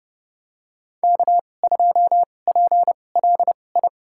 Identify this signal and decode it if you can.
Morse code, heard as a keyed tone, about 30 wpm, spaced as for 22 wpm, 700 Hz. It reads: K2PLI
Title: USA Call Signs - 5 Characters - 2x Character Spacing - 30wpm